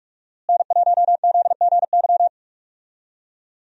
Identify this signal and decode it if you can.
Morse code, heard as a keyed tone, about 43 wpm, 700 Hz. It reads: N1ZGY
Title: USA Call Signs - 5 Characters - Repeated 2x - 45wpm